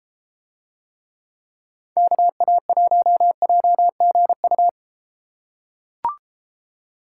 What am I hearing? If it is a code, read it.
Morse code, heard as a keyed tone, about 32 wpm, 700 Hz. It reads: KA1JGU